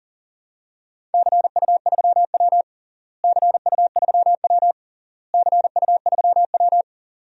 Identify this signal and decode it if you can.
Morse code, heard as a keyed tone, about 40 wpm, 700 Hz. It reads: CU3W CU3W CU3W
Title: DXCC-100 Call Signs - 4 Characters - Repeated 3x - 40wpm